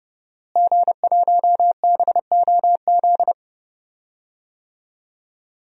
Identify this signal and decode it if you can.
Morse code, heard as a keyed tone, about 30 wpm, 700 Hz. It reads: G1BOZ
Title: DXCC-100 Call Signs - 5 Characters - Repeated 3x - 30wpm